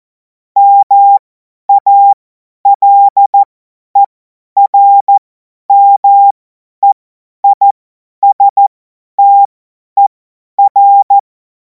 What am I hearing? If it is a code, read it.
Morse code, heard as a keyed tone, about 14 wpm, spaced as for 10 wpm, 800 Hz. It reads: MALERMEISTER
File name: Morsekurs / Witze / Witz 24. Februar 2026 50 F70